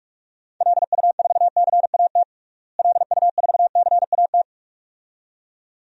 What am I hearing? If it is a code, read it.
Morse code, heard as a keyed tone, about 46 wpm, 700 Hz. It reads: LU4CAT LU4CAT
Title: DXCC-100 Call Signs - 6 Characters - Repeated 2x - 45wpm